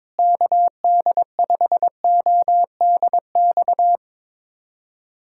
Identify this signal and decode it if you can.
Morse code, heard as a keyed tone, about 22 wpm, 700 Hz. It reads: KD5ODX
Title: USA Callsign DE Callsign - 6 Characters - 22wpm